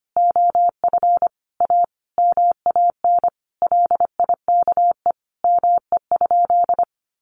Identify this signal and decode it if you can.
Morse code, heard as a keyed tone, about 25 wpm, 700 Hz. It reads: OF A MAN LIKE ME?
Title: Sentences from Top 1200 Words 25wpm